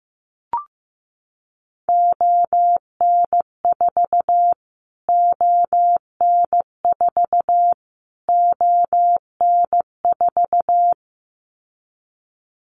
Morse code, heard as a keyed tone, about 15 wpm, 700 Hz. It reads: ON4 ON4 ON4
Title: Callsign Prefixes for Top 25 DXCC - Repeated 3x - 15wpm